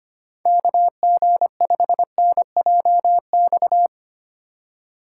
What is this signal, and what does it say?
Morse code, heard as a keyed tone, about 30 wpm, 700 Hz. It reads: KG5NJX
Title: Speed Racing - USA 2x3 Call Signs 25wpm